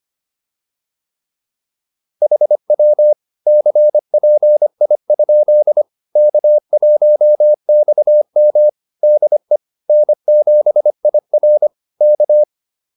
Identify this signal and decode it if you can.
Morse code, heard as a keyed tone, about 25 wpm, 600 Hz. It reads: HW CPI? K1XM DE N7IR K